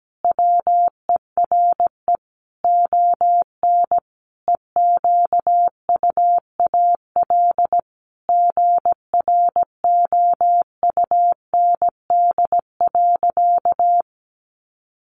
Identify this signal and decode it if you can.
Morse code, heard as a keyed tone, about 17 wpm, 700 Hz. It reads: WERE ON EQUAL GROUND.
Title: Sentences from Top 1400 Words 17wpm